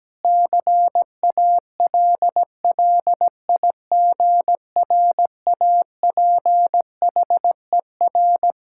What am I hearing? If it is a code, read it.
Morse code, heard as a keyed tone, about 17 wpm, 700 Hz. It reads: CALLIGRAPHER
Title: Pangram Challenge v2 - 17wpm